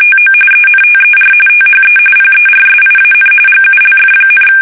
RTTY, about 45.45 baud. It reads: TO ENTER THE PERSIAN GULF,
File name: fsk850 / battleshipnj